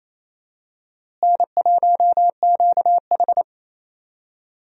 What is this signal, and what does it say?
Morse code, heard as a keyed tone, about 28 wpm, 700 Hz. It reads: N1QH
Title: USA Call Signs - 4 Characters - Repeated 3x - 28wpm